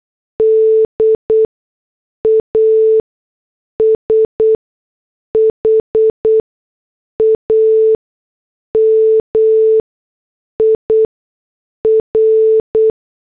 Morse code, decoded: DASHAMIR